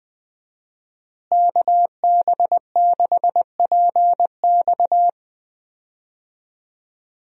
Morse code, heard as a keyed tone, about 21 wpm, 700 Hz. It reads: KB6PX
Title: USA Call Signs - 5 Characters - Repeated 3x - 20wpm